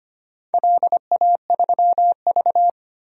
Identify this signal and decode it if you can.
Morse code, heard as a keyed tone, about 25 wpm, 700 Hz. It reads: LA3V
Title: DXCC-100 Call Signs - 4 Characters - 25wpm